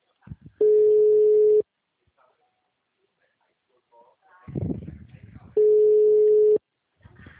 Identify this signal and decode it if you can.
Ringback tone (Europe, 425 Hz), with speech.